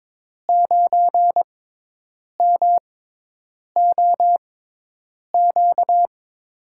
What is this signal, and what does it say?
Morse code, heard as a keyed tone, about 22 wpm, spaced as for 8 wpm, 700 Hz. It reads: 9MOQ
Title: USA Call Signs - 6 Characters - 6x Character Spacing - 22wpm